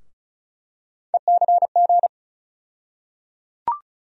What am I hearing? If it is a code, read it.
Morse code, heard as a keyed tone, about 36 wpm, 700 Hz. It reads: ECG
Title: USA Call Sign Postfixes - 3 Letters 35wpm